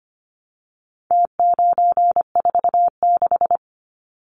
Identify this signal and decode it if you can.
Morse code, heard as a keyed tone, about 25 wpm, 700 Hz. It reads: T946
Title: Numbers - Contest Style - 4-Digit 25wpm